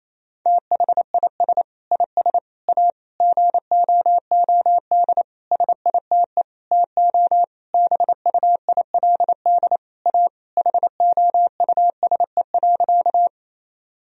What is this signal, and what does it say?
Morse code, heard as a keyed tone, about 28 wpm, 700 Hz. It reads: THIS IS A GOOD SITE TO BUILD A HOUSE.
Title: Sentences from Top 1000 Words - Review Part 4 - 28wpm